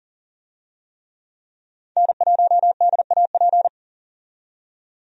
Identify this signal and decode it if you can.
Morse code, heard as a keyed tone, about 38 wpm, 700 Hz. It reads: N1DAP
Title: USA Call Signs - 5 Characters - 40wpm